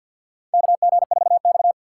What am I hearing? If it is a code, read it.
Morse code, heard as a keyed tone, about 52 wpm, 700 Hz. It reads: KG4X
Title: USA Call Signs - 4 Characters - 3x Character Spacing - 50wpm